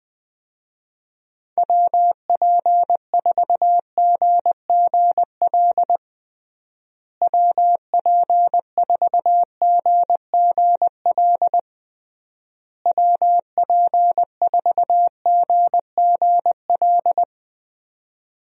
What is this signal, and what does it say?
Morse code, heard as a keyed tone, about 20 wpm, 700 Hz. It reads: WP4GGL WP4GGL WP4GGL
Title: USA Call Signs - 6 Characters - Repeated 3x - 20wpm